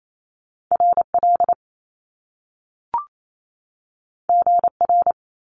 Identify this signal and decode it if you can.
Morse code, heard as a keyed tone, about 28 wpm, 700 Hz. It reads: RL GR
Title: Top Two Letter Combinations in Words 28wpm